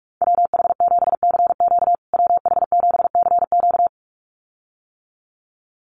Morse code, heard as a keyed tone, about 45 wpm, 700 Hz. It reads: WH7CQ WH7CQ
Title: USA Call Signs Repeated 2x 45wpm